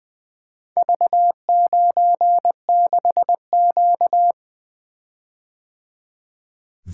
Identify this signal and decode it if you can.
Morse code, heard as a keyed tone, about 20 wpm, 700 Hz. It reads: V96Q
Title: Four Letter-Number Combinations - 20wpm